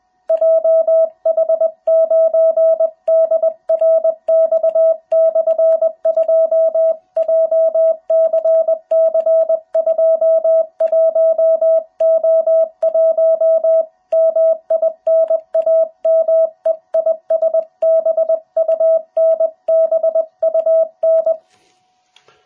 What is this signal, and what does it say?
Morse code, heard as a keyed tone, about 21 wpm, 660 Hz. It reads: JH9DRX/2 JCC21O1 MI NAME IS BUN BUN
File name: jh8cbh / 2021cw